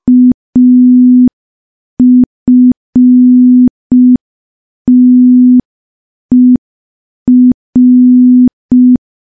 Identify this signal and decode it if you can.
Morse code, heard as a keyed tone, about 5 wpm, 250 Hz. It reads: AFTER
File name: morse / morse-training-05WPM